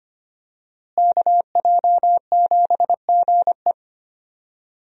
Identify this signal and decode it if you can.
Morse code, heard as a keyed tone, about 25 wpm, 700 Hz. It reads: KJ7GE
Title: USA Call Signs - 5 Characters - Repeated 3x - 25wpm